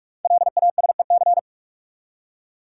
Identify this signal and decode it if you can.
Morse code, heard as a keyed tone, about 44 wpm, 690 Hz. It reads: LASEC
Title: Code Groups - 5 Letters - 45wpm